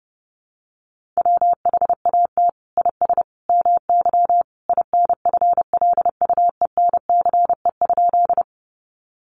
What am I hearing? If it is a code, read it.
Morse code, heard as a keyed tone, about 30 wpm, 700 Hz. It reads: WHAT IS MY INFLUENCE?
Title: Sentences from Top 1100 Words 30wpm